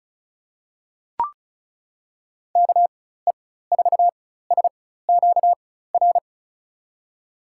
Morse code, heard as a keyed tone, about 36 wpm, spaced as for 16 wpm, 700 Hz. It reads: KE4SQR
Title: USA Call Signs - 6 Characters - 4x Character Spacing - 35wpm